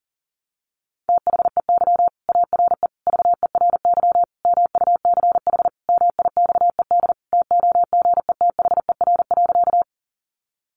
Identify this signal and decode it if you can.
Morse code, heard as a keyed tone, about 40 wpm, 700 Hz. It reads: THEY ARE VERY MUCH MIXED TOGETHER.
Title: Sentences from Top 1500 Words 40wpm